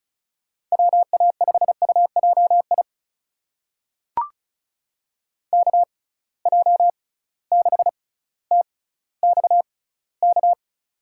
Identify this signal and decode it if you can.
Morse code, heard as a keyed tone, about 36 wpm, spaced as for 18 wpm, 700 Hz. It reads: WA5UJI KJ6TXK
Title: USA Call Signs - 6 Characters - 6x Character Spacing - 35wpm